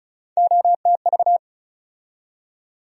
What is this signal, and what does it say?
Morse code, heard as a keyed tone, about 36 wpm, 700 Hz. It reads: OTV
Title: USA Call Sign Postfixes - 3 Letters 35wpm